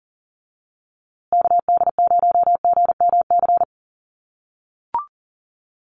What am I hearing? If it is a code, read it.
Morse code, heard as a keyed tone, about 40 wpm, 700 Hz. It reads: KD0GMC
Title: USA Call Signs Repeated 3x 40wpm